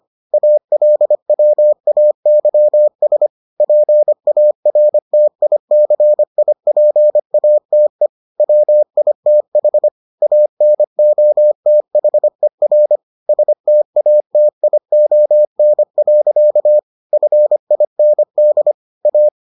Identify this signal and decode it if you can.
Morse code, heard as a keyed tone, about 25 wpm, 600 Hz. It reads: ALWAYS PARTICIPATE WITH ANOTHER STATION. FIND A